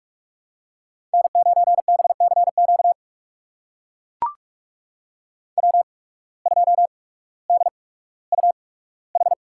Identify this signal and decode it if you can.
Morse code, heard as a keyed tone, about 44 wpm, spaced as for 19 wpm, 700 Hz. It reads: N9BCQ W2DUH
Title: USA Call Signs - 5 Characters - 8x Character Spacing - 45wpm